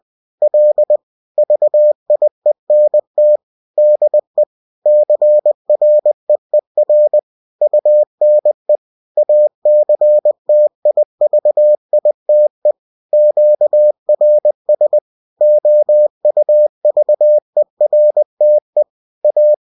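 Morse code, heard as a keyed tone, about 20 wpm, 600 Hz. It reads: L VIENT DE CREER UNE ACTIVITE QRS OUVERTE A
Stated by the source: UFT bulletin sent in CW.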